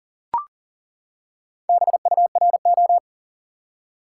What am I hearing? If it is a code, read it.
Morse code, heard as a keyed tone, about 40 wpm, 700 Hz. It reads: BURO
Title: Top 180 QSO Elements 40wpm